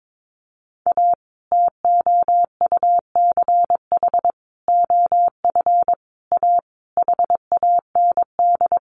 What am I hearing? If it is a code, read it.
Morse code, heard as a keyed tone, about 22 wpm, 700 Hz. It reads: A TOUCH OF A HAND
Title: A Sound Of Thunder 22wpm